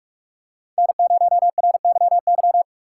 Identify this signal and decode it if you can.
Morse code, heard as a keyed tone, about 48 wpm, 700 Hz. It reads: N0RYY
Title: USA Call Signs - 5 Characters - 4x Character Spacing - 45wpm